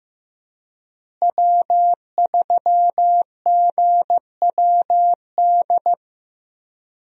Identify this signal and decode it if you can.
Morse code, heard as a keyed tone, about 15 wpm, 700 Hz. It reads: W3GWD
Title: USA Call Signs - 5 Characters - Repeated 2x - 15wpm